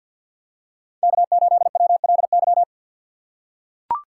Morse code, heard as a keyed tone, about 45 wpm, 700 Hz. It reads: K8WLY
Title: USA Call Signs - 5 Characters - 2x Character Spacing - 50wpm